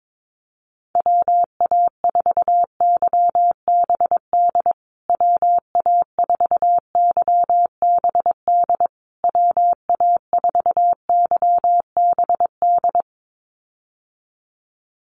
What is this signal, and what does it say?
Morse code, heard as a keyed tone, about 22 wpm, 700 Hz. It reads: WA4YBD WA4YBD WA4YBD
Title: USA Call Signs Repeated 3x 22wpm